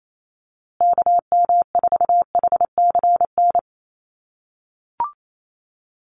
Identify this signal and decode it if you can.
Morse code, heard as a keyed tone, about 28 wpm, 700 Hz. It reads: KM4HCN
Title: USA Call Signs Repeated 2x 28wpm